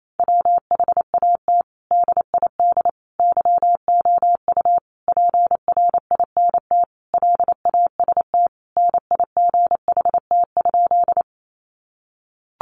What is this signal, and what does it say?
Morse code, heard as a keyed tone, about 28 wpm, 700 Hz. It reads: WHAT DID YOU PRINT LAST NIGHT?
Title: Sentences from Top 1400 Words 28wpm